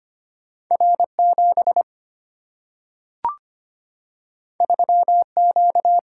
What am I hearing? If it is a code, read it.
Morse code, heard as a keyed tone, about 25 wpm, 700 Hz. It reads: R 7 3 Q
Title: Single-Letter Single-Number 25wpm